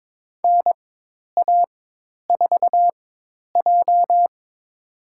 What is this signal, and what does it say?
Morse code, heard as a keyed tone, about 22 wpm, spaced as for 10 wpm, 700 Hz. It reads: NA4J
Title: USA Call Signs - 4 Characters - 4x Character Spacing - 22wpm